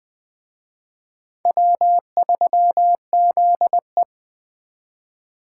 Morse code, heard as a keyed tone, about 24 wpm, 700 Hz. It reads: W3ZE
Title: Speed Racing - USA 1x2 Call Signs 20wpm